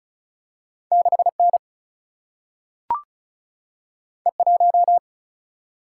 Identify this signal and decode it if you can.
Morse code, heard as a keyed tone, about 36 wpm, 700 Hz. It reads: 6N E1
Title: Single-Letter Single-Number 35wpm